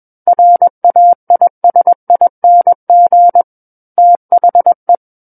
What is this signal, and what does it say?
Morse code, heard as a keyed tone, about 21 wpm, 700 Hz. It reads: RAISING THE
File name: Book One / 12-Chapter 12 - 21 WPM